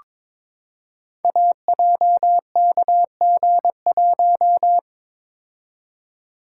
Morse code, heard as a keyed tone, about 22 wpm, 700 Hz. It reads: AJKG1
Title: Code Groups - 5 Letter:Numbers - 22wpm